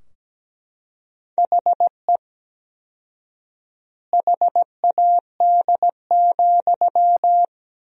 Morse code, heard as a keyed tone, about 17 wpm, 700 Hz. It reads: HE HAD,
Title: 4x Word-Spacing - Sentences from Top 500 Words - 17wpm